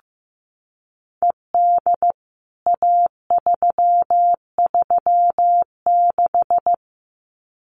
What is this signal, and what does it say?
Morse code, heard as a keyed tone, about 15 wpm, 700 Hz. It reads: ED A336
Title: CWT Exchange 15wpm